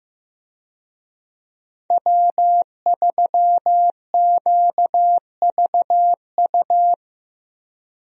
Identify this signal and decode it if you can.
Morse code, heard as a keyed tone, about 15 wpm, 700 Hz. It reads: W3QVU
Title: USA Call Signs - 5 Characters - 15wpm